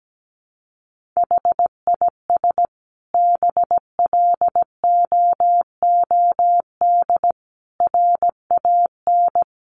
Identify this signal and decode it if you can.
Morse code, heard as a keyed tone, about 17 wpm, 700 Hz. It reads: HIS BLOOD RAN
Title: Sentences from Top 1100 Words 17wpm